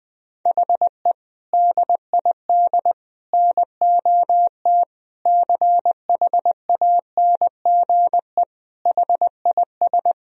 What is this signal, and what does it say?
Morse code, heard as a keyed tone, about 21 wpm, 700 Hz. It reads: HE DID NOT CHANGE HIS
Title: Sentences from Top 1000 Words - Review Part 1 - 20wpm